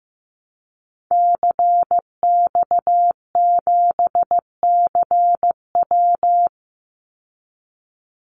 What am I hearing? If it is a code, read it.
Morse code, heard as a keyed tone, about 15 wpm, 700 Hz. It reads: CX7CW
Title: Call Signs Repeated 3x - 15wpm